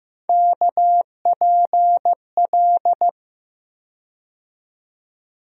Morse code, heard as a keyed tone, about 15 wpm, 700 Hz. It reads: KPL
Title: USA Call Sign Postfixes - 3 Letters 15wpm